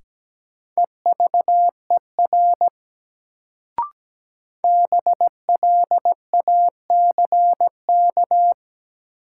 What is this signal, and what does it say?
Morse code, heard as a keyed tone, about 17 wpm, 700 Hz. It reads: EVER BLACK
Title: Sentences from Top 400 Words - New Word Drill - 17wpm